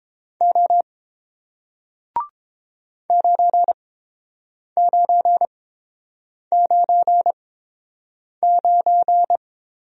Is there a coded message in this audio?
Yes: Morse code from 0.4 to 9.4 s, heard as a keyed tone, about 30 wpm, spaced as for 7 wpm, 700 Hz. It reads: O 9999